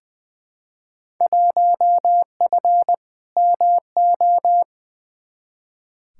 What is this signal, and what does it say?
Morse code, heard as a keyed tone, about 21 wpm, 700 Hz. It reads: 1F MO
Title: Field Day Exchanges Encore - 20wpm